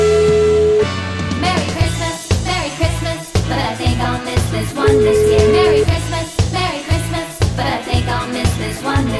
Ringback tone (Europe, 425 Hz), with noise.